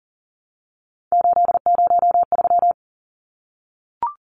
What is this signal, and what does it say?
Morse code, heard as a keyed tone, about 40 wpm, 700 Hz. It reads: 803
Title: Numbers - 3-Digit 40wpm